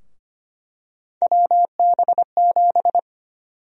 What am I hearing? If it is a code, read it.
Morse code, heard as a keyed tone, about 25 wpm, 700 Hz. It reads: WB7
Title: USA Call Sign Prefixes - 25wpm